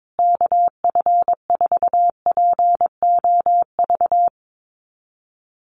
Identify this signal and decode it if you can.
Morse code, heard as a keyed tone, about 22 wpm, 700 Hz. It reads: KF4POV
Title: USA Call Signs Repeated 3x 22wpm